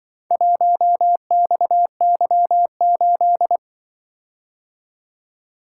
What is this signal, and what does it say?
Morse code, heard as a keyed tone, about 24 wpm, 700 Hz. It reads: 1XY8
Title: Speed-Racing - Four Letter-Number Combinations - 20wpm